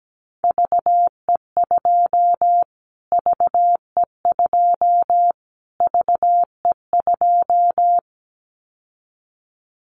Morse code, heard as a keyed tone, about 17 wpm, 700 Hz. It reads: VE2 VE2 VE2
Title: Callsign Prefixes for Top 25 DXCC - Repeated 3x - 17wpm